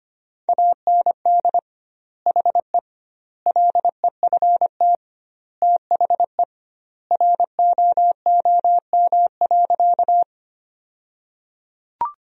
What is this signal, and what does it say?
Morse code, heard as a keyed tone, about 25 wpm, 700 Hz. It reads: AND HE LEFT THE ROOM.
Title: 2x Word-Spacing - Sentences from Top 300 Words - 25wpm